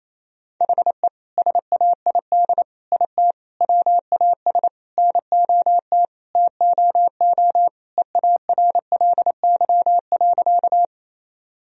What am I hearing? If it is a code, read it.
Morse code, heard as a keyed tone, about 28 wpm, 700 Hz. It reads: HE SAID IT WAS NOT TOO EARLY.
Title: Sentences from Top 1000 Words - Review Part 2 - 28wpm